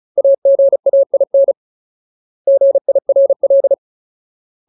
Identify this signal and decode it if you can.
Morse code, heard as a keyed tone, about 36 wpm, 550 Hz. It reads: AGAIN GIRL
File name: Ultimate Word List / 35WPM Ultimate